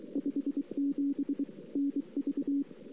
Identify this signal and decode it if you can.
Morse code, heard as a keyed tone, about 23 wpm, 280 Hz. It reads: 57 NV
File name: CW / 2008-ss-cw-k5tr-op-n5rz -23:00